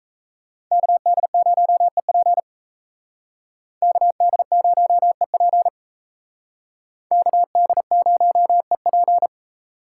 Morse code, heard as a keyed tone, about 36 wpm, 700 Hz. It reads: KD0EP KD0EP KD0EP